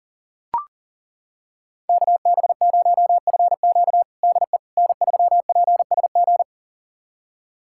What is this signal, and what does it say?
Morse code, heard as a keyed tone, about 40 wpm, 700 Hz. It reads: KB0FQ DE N3PSG
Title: USA Callsign DE Callsign - 5 Characters - 40wpm